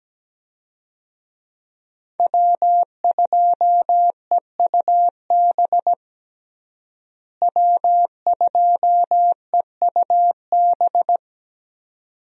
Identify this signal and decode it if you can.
Morse code, heard as a keyed tone, about 17 wpm, 700 Hz. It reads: W2EUB W2EUB
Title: USA Call Signs - 5 Characters - Repeated 2x - 17wpm